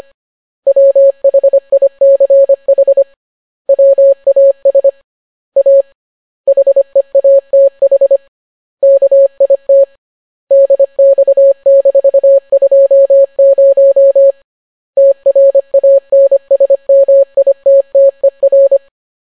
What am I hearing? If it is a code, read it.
Morse code, heard as a keyed tone, about 25 wpm, 550 Hz. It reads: WHICH WAS A HEATH KIT DX-20 TRANSMITTER